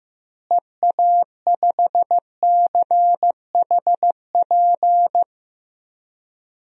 Morse code, heard as a keyed tone, about 15 wpm, 700 Hz. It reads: EA5CHP
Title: DXCC-100 Call Signs - 6 Characters - Repeated 2x - 15wpm